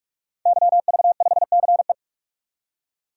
Morse code, heard as a keyed tone, about 46 wpm, 700 Hz. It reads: YV5CE
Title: DXCC-100 Call Signs - 5 Characters - 45wpm